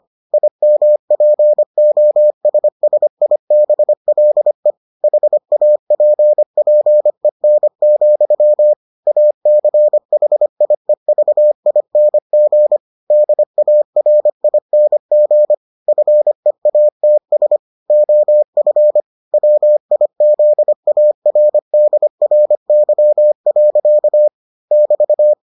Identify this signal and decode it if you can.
Morse code, heard as a keyed tone, about 25 wpm, 600 Hz. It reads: IMPOSSIBLE HAPPEN, ACHIEVING DARING FEATS OF WIZARDRY. <BT>